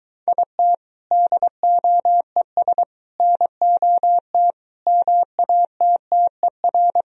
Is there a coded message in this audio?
Yes: Morse code from 0.3 to 7.0 s, heard as a keyed tone, about 23 wpm, 700 Hz. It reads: IT DOES NOT MATTER